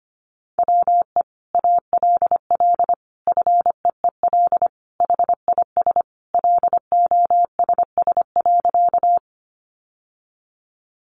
Morse code, heard as a keyed tone, about 25 wpm, 700 Hz. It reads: WE ALL FEEL HIS LOSS.
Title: Sentences from Top 1400 Words 25wpm